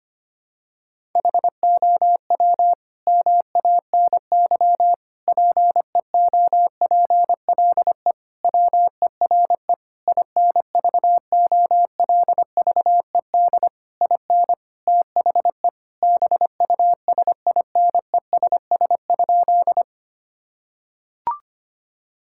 Morse code, heard as a keyed tone, about 25 wpm, 700 Hz. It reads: HOW MANY PEOPLE WERE INVOLVED IN THE BUSINESS?
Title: Sentences from Top 1000 Words - Review Part 5 - 25wpm